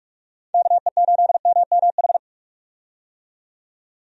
Morse code, heard as a keyed tone, about 42 wpm, 700 Hz. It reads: KE8MMH
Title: USA Call Signs - 6 Characters - 45wpm